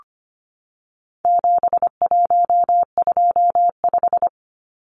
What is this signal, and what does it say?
Morse code, heard as a keyed tone, about 25 wpm, 700 Hz. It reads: 7125
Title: Numbers - 4-Digit 25wpm